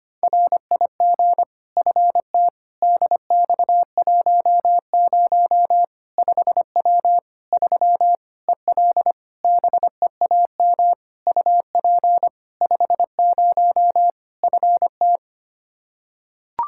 Morse code, heard as a keyed tone, about 25 wpm, 700 Hz. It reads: RIG FT DX10 5W 3 EL BEAM UP 50 FT
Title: Practice QSO - Exchange 2 - Info - 25wpm